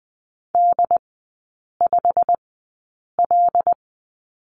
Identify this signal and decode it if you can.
Morse code, heard as a keyed tone, about 20 wpm, spaced as for 8 wpm, 700 Hz. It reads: D5L
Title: 191 Lesson 31e - Letter v - Three Characters 20wpm